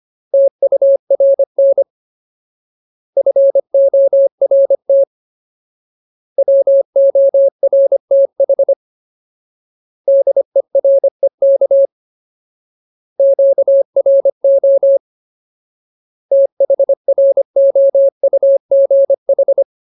Morse code, heard as a keyed tone, about 25 wpm, 550 Hz. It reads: TURN FORT WORTH DEREK QRO THROUGH
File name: Ultimate Word List / 25WPM Ultimate